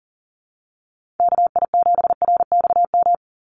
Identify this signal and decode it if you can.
Morse code, heard as a keyed tone, about 40 wpm, 690 Hz. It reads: KI7RXM
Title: USA Call Signs 40wpm